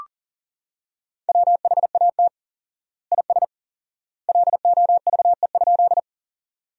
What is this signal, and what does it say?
Morse code, heard as a keyed tone, about 40 wpm, 700 Hz. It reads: WHAT IS LOVE?
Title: 4x Word-Spacing - Sentences from Top 400 Words - 40wpm